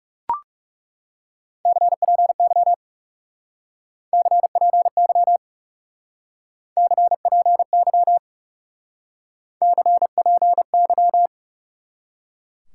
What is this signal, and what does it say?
Morse code, heard as a keyed tone, about 38 wpm, 700 Hz. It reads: CPY CPY CPY CPY